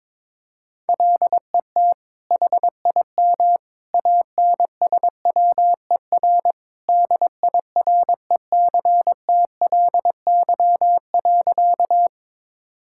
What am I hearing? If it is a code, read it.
Morse code, heard as a keyed tone, about 22 wpm, 700 Hz. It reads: LET HIM ANSWER DIRECTLY.
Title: Sentences from Top 1000 Words - Review Part 5 - 22wpm